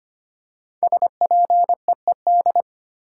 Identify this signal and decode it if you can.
Morse code, heard as a keyed tone, about 25 wpm, 700 Hz. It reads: SPEED
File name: 5 Letter Words - First 3 Letters Given - 25wpm